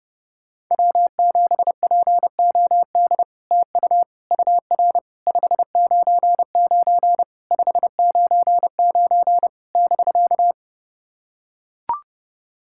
Morse code, heard as a keyed tone, about 29 wpm, 700 Hz. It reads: W7POD TU UR 599 599 <BK>
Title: SOTA QSO Practice 30wpm